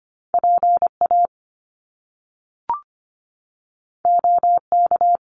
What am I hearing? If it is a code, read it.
Morse code, heard as a keyed tone, about 25 wpm, 700 Hz. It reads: PA OK
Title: Top Two Letter Combinations in Words 25wpm